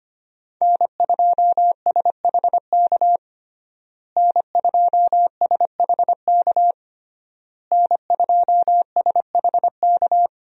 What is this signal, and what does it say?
Morse code, heard as a keyed tone, about 25 wpm, 700 Hz. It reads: N2SHK N2SHK N2SHK
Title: USA Call Signs - 5 Characters - Repeated 3x - 25wpm